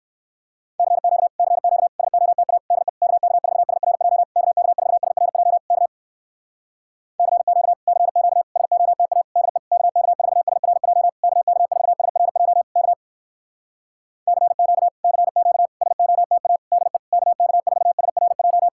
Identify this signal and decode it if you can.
Morse code, heard as a keyed tone, about 59 wpm, 700 Hz. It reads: CQ CQ SOTA DE KK4SRJ KK4SRJ K CQ CQ SOTA DE KK4SRJ KK4SRJ K CQ CQ SOTA DE KK4SRJ